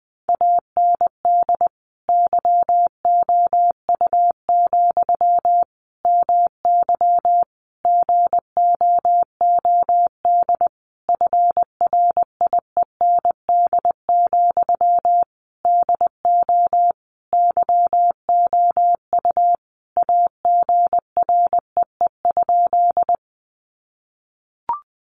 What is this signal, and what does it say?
Morse code, heard as a keyed tone, about 20 wpm, 700 Hz. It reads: AND YOU, MY GOOD FRIEND, DO YOU AGREE?
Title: Sentences from Top 1100 Words 20wpm